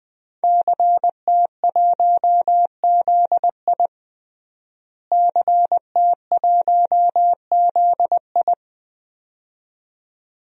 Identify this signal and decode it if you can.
Morse code, heard as a keyed tone, about 20 wpm, 700 Hz. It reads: CT1ZI CT1ZI
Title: DXCC-100 Call Signs - 5 Characters - Repeated 2x - 20wpm